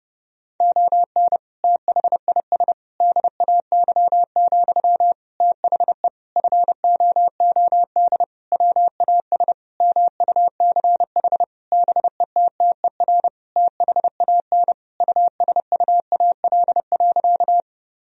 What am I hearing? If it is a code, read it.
Morse code, heard as a keyed tone, about 32 wpm, 700 Hz. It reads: ON THIS DAY, THE FOOD WAS MUCH BETTER THAN USUAL.
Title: Sentences from Top 1000 Words - Review Part 2 - 30wpm